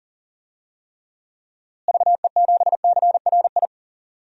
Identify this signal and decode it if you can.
Morse code, heard as a keyed tone, about 40 wpm, 700 Hz. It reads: VE7CRI
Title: DXCC-100 Call Signs - 6 Characters - Repeated 3x - 40wpm